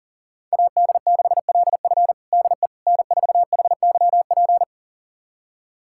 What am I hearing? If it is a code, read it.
Morse code, heard as a keyed tone, about 40 wpm, 700 Hz. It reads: AD6LF DE N4HYP
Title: USA Callsign DE Callsign - 5 Characters - 40wpm